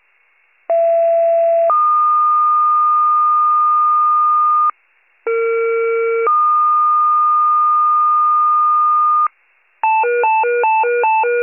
Two-tone page, heard as a one-second tone, then a three-second tone, about 670 then 1220 Hz, about 460 then 1220 Hz. It is heard from 0.7 to 9.3 s.
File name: pager / 21 RESCUE / wcalert